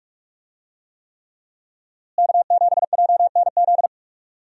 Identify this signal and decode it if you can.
Morse code, heard as a keyed tone, about 46 wpm, 700 Hz. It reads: K7JNZ
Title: USA Call Signs - 5 Characters - Repeated 3x - 45wpm